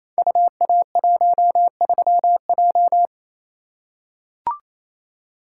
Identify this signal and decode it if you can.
Morse code, heard as a keyed tone, about 28 wpm, 700 Hz. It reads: UA13J
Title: Code Groups - 5 Letter:Numbers - 28wpm